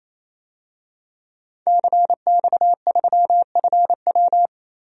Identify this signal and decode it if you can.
Morse code, heard as a keyed tone, about 28 wpm, 700 Hz. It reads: CX3FW
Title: DXCC-100 Call Signs - 5 Characters - 28wpm